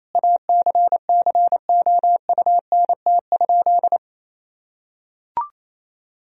Morse code, heard as a keyed tone, about 28 wpm, 700 Hz. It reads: ACCOUNT?
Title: Sentences from Top 1000 Words - Review Part 4 - 28wpm